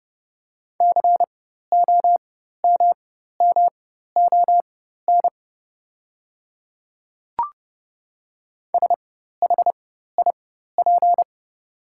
Morse code, heard as a keyed tone, about 30 wpm, spaced as for 14 wpm, 690 Hz. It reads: COMMON SHIP
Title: 4x Character Spacing - Top 500 Words - 30wpm